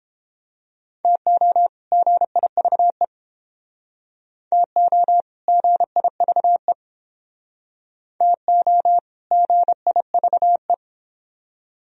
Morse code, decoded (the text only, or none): TO GIVE TO GIVE TO GIVE